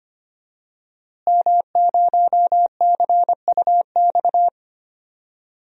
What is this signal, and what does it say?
Morse code, heard as a keyed tone, about 25 wpm, 700 Hz. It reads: M0CUX
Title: DXCC-100 Call Signs - 5 Characters - Repeated 3x - 25wpm